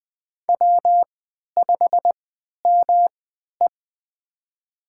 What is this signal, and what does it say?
Morse code, heard as a keyed tone, about 20 wpm, spaced as for 11 wpm, 700 Hz. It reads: W5ME
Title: USA Call Signs - 4 Characters - 3x Character Spacing - 20wpm